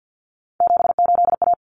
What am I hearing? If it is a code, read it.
Morse code, heard as a keyed tone, about 50 wpm, 700 Hz. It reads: 78A
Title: Common Cut Numbers - 3-Digit 50wpm